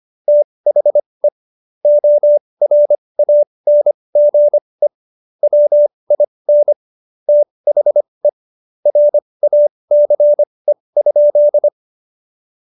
Morse code, heard as a keyed tone, about 25 wpm, spaced as for 20 wpm, 590 Hz. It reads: THE ORANGE WIN THE RACE?
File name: Jokes-25-char-20-eff